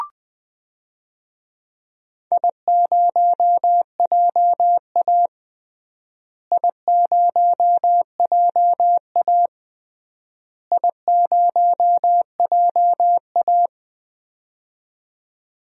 Morse code, heard as a keyed tone, about 20 wpm, 700 Hz. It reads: I0JA I0JA I0JA
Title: DXCC-100 Call Signs - 4 Characters - Repeated 3x - 20wpm